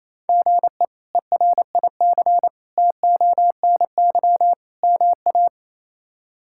Morse code, heard as a keyed tone, about 28 wpm, 700 Hz. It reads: GE ERIC TONY MA
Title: K1USN SST QSO Practice - 28wpm